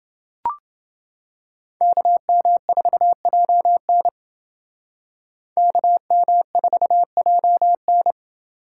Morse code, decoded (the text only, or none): KM4JN KM4JN